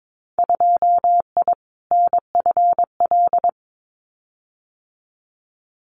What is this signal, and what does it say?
Morse code, heard as a keyed tone, about 22 wpm, 700 Hz. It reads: 2I NFL
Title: Winter Field Day Exchanges 22wpm